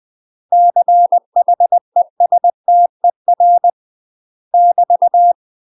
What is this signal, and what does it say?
Morse code, heard as a keyed tone, about 20 wpm, 700 Hz. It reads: CHESTER <BT>
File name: UK MASTER QSOs TRAINING / 20WPM / UK MASTER QSOs 0058